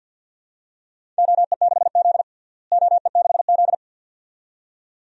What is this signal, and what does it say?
Morse code, heard as a keyed tone, about 51 wpm, 700 Hz. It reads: OE6Z OE6Z
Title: DXCC-100 Call Signs - 4 Characters - Repeated 2x - 50wpm